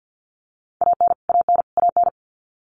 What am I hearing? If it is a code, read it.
Morse code, heard as a keyed tone, about 50 wpm, 700 Hz. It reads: AN AN AN
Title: Top 300 Words Repeated 50wpm